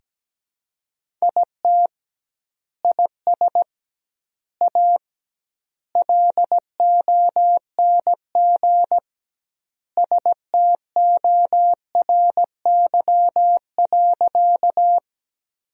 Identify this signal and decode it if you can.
Morse code, heard as a keyed tone, about 17 wpm, 700 Hz. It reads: IT IS A LONG STORY.
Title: 2x Word-Spacing - Sentences from Top 300 Words - 17wpm